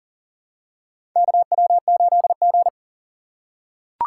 Morse code, heard as a keyed tone, about 40 wpm, 700 Hz. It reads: KW8G
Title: USA Call Signs - 4 Characters - 2x Character Spacing - 40wpm